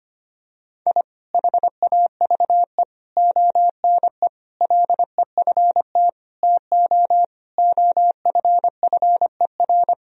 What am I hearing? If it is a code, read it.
Morse code, heard as a keyed tone, about 25 wpm, 700 Hz. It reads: I HAVE ONE LEFT TO OFFER
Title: Sentences from Top 1000 Words - Review Part 4 - 25wpm